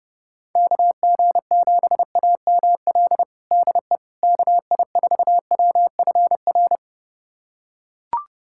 Morse code, heard as a keyed tone, about 31 wpm, 700 Hz. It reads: KG7AML DE KI4WFR
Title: USA Callsign DE Callsign 30wpm